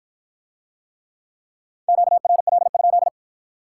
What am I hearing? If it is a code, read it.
Morse code, heard as a keyed tone, about 54 wpm, 700 Hz. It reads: QRL?